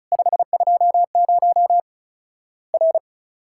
Morse code, heard as a keyed tone, about 34 wpm, 700 Hz. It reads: 520 R
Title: Worked All Europe DX Contest - QTC Exchange - 35wpm